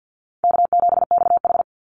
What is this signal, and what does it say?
Morse code, heard as a keyed tone, about 50 wpm, 700 Hz. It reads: K7XH
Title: Easy Call Signs Repeated 3x - 50wpm